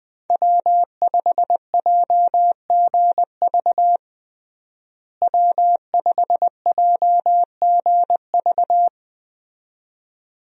Morse code, heard as a keyed tone, about 20 wpm, 700 Hz. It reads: W5JGV W5JGV
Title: USA Call Signs - 5 Characters - Repeated 2x - 20wpm